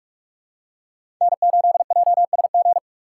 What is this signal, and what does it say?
Morse code, heard as a keyed tone, about 46 wpm, 700 Hz. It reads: N8JSG
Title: USA Call Signs - 5 Characters - 4x Character Spacing - 45wpm